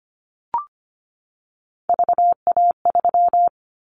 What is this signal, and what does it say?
Morse code, heard as a keyed tone, about 25 wpm, 700 Hz. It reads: VA3
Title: Callsign Prefixes for Top 50 DXCC - 25wpm